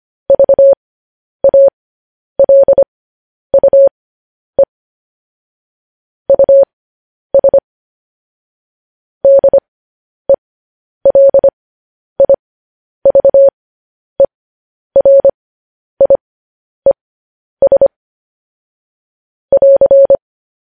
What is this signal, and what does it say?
Morse code, heard as a keyed tone, about 25 wpm, spaced as for 10 wpm, 550 Hz. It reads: VALUE US DELIVERIES <AR>
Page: News Headlines in Morse Code at 10 WPM